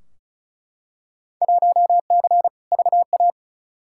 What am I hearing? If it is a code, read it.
Morse code, heard as a keyed tone, about 36 wpm, 690 Hz. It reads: 1C VA
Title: Field Day Exchanges Encore - 35wpm